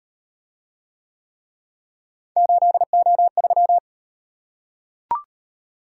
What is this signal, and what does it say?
Morse code, heard as a keyed tone, about 38 wpm, 700 Hz. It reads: 8O3